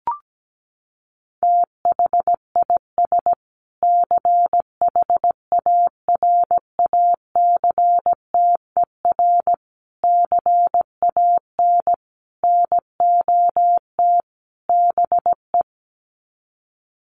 Morse code, heard as a keyed tone, about 17 wpm, 700 Hz. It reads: THIS CHARACTER CAN NOT BE
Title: Sets of 5 Words Encore2 17wpm